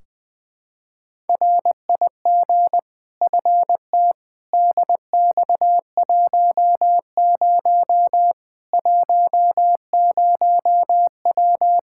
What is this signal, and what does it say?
Morse code, heard as a keyed tone, about 20 wpm, 700 Hz. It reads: RIG FT DX10 10W
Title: K1USN SST QSO Practice 20wpm